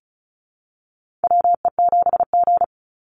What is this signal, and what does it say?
Morse code, heard as a keyed tone, about 34 wpm, 700 Hz. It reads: WE7G
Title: Call Signs Repeated 3x - 35wpm